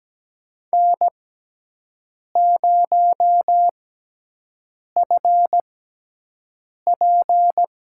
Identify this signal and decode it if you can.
Morse code, heard as a keyed tone, about 17 wpm, spaced as for 6 wpm, 700 Hz. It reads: N0FP
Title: USA Call Signs - 6x Character Spacing - 17wpm